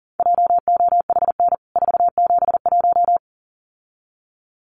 Morse code, heard as a keyed tone, about 37 wpm, 700 Hz. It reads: JOHN 471
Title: CWT Exchange 40wpm